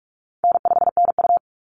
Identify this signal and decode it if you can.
Morse code, heard as a keyed tone, about 44 wpm, 700 Hz. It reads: N5NU
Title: Call Signs - 45wpm